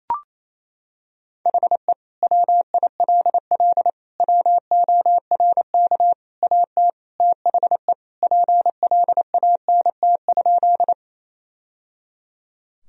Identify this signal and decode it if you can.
Morse code, heard as a keyed tone, about 28 wpm, 700 Hz. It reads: HE WILL WORK AT THE PLANT?
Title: Sentences from Top 1000 Words - Review Part 1 - 28wpm